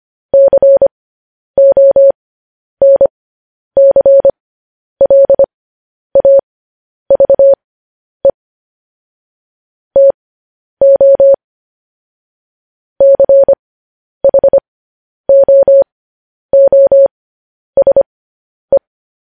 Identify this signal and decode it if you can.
Morse code, heard as a keyed tone, about 25 wpm, spaced as for 10 wpm, 550 Hz. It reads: CONCLAVE TO CHOOSE
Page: News Headlines in Morse Code at 10 WPM